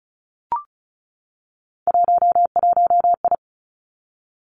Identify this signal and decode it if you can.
Morse code, heard as a keyed tone, about 36 wpm, 700 Hz. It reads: 11I
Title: Three Challenging Letter-Number Combinations 35wpm